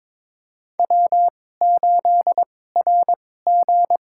Morse code, heard as a keyed tone, about 21 wpm, spaced as for 16 wpm, 700 Hz. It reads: W8RG
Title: USA Call Signs - 4 Characters - 2x Character Spacing - 22wpm